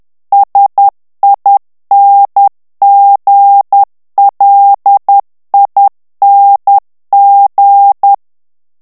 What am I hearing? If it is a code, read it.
Morse code, heard as a keyed tone, about 10 wpm, 800 Hz. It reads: SINGLING
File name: morse mp3 / ANGLAIS texte clair 2 vitesse 10 mots minute